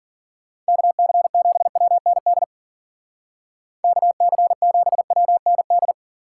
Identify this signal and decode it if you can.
Morse code, heard as a keyed tone, about 50 wpm, 700 Hz. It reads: KC7WND KC7WND